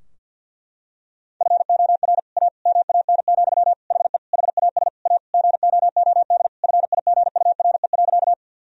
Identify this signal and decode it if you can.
Morse code, heard as a keyed tone, about 52 wpm, 700 Hz. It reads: FOR A MAN, HE HAS A GOOD FIGURE.
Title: Sentences from Top 1000 Words - Review Part 2 - 50wpm